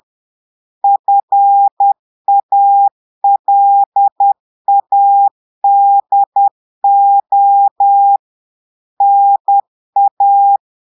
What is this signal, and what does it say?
Morse code, heard as a keyed tone, about 10 wpm, 800 Hz. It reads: FALADO NA
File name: portugais texte en clair 2 vitesse 10 mots minutes